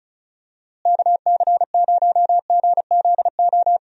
Morse code, heard as a keyed tone, about 36 wpm, 700 Hz. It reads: KC0GZO
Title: USA Call Signs - 6 Characters - Repeated 2x - 35wpm